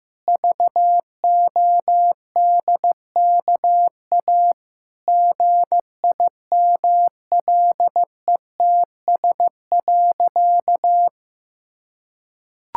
Morse code, heard as a keyed tone, about 15 wpm, 700 Hz. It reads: VODKA GIMLETS.
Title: Pangram Challenge v2 - 15wpm